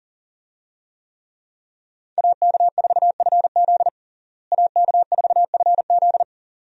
Morse code, heard as a keyed tone, about 40 wpm, 700 Hz. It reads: AK4FZ AK4FZ
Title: USA Call Signs - 5 Characters - Repeated 2x - 40wpm